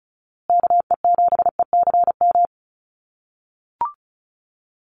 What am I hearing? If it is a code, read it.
Morse code, heard as a keyed tone, about 34 wpm, 700 Hz. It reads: KE7ECM
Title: USA Call Signs Repeated 2x 35wpm